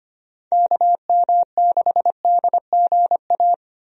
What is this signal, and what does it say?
Morse code, heard as a keyed tone, about 25 wpm, 700 Hz. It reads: KM6DGA
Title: USA Call Signs - 6 Characters - 25wpm